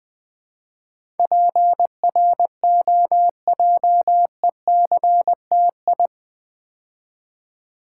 Morse code, heard as a keyed tone, about 20 wpm, 700 Hz. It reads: PROJECTI
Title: Build A Word - 4 Letters At A Time - 10-Letter Words - 20wpm